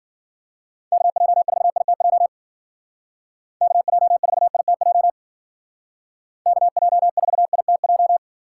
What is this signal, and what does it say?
Morse code, heard as a keyed tone, about 51 wpm, 700 Hz. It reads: KJ4ITJ KJ4ITJ KJ4ITJ